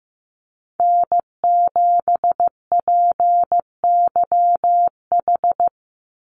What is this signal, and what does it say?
Morse code, heard as a keyed tone, about 15 wpm, 700 Hz. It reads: N7PYH
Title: USA Call Signs 15wpm